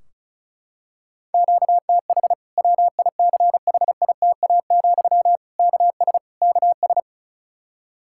Morse code, heard as a keyed tone, about 34 wpm, 700 Hz. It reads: QTH WICHITA, KS KS
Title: Practice QSO - Exchange 1 - Beginning - 35wpm